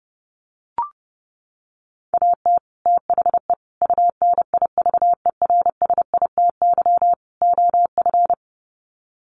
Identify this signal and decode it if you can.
Morse code, heard as a keyed tone, about 30 wpm, 700 Hz. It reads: AT THE UNIVERSITY OF
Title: Sets of 4 Words 30wpm